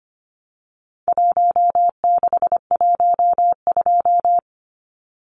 Morse code, heard as a keyed tone, about 25 wpm, 700 Hz. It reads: 1612
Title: Numbers - Contest Style - 4-Digit 25wpm